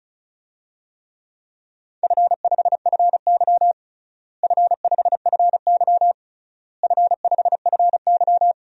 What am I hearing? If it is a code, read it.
Morse code, heard as a keyed tone, about 36 wpm, 700 Hz. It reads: F5FY F5FY F5FY
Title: DXCC-100 Call Signs - 4 Characters - Repeated 3x - 35wpm